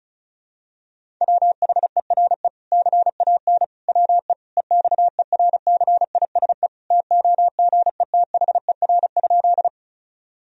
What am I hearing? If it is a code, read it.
Morse code, heard as a keyed tone, about 36 wpm, 700 Hz. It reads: WHERE CAN WE EXERCISE TOGETHER?
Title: Sentences from Top 1000 Words - Review Part 4 - 35wpm